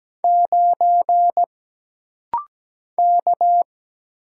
Morse code, heard as a keyed tone, about 17 wpm, 700 Hz. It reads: 9 K
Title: Single Letter-Number - Rapid-Fire 17wpm